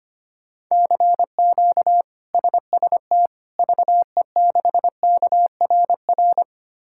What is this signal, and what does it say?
Morse code, heard as a keyed tone, about 25 wpm, 700 Hz. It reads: CQ SST VE6KRR
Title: K1USN SST QSO Practice - 25wpm